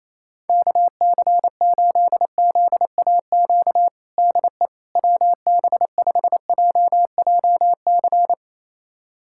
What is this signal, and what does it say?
Morse code, heard as a keyed tone, about 28 wpm, 700 Hz. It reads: KC8ZAQ DE WB5JJC
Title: USA Callsign DE Callsign 28wpm